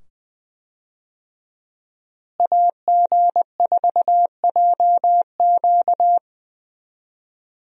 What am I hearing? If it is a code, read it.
Morse code, heard as a keyed tone, about 20 wpm, 700 Hz. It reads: AG4JQ
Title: USA Call Signs - 5 Characters - Repeated 2x - 20wpm